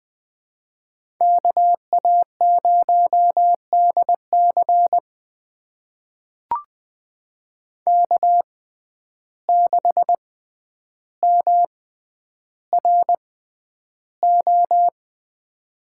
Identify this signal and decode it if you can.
Morse code, heard as a keyed tone, about 20 wpm, spaced as for 10 wpm, 700 Hz. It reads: KA0DC K6MRO
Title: USA Call Signs - 5 Characters - 6x Character Spacing - 20wpm